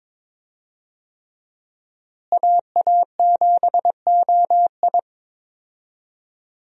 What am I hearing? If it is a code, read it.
Morse code, heard as a keyed tone, about 21 wpm, 700 Hz. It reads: AA7OI
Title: USA Call Signs - 5 Characters - Repeated 3x - 22wpm